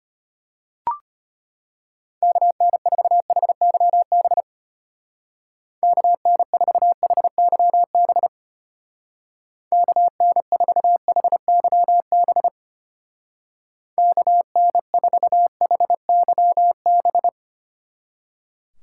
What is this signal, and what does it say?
Morse code, heard as a keyed tone, about 32 wpm, 700 Hz. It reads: KN4HYB KN4HYB KN4HYB KN4HYB